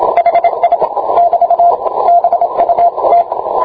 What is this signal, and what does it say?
Morse code, heard as a keyed tone, about 26 wpm, 690 Hz, with radio noise. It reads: HI <BT> BUT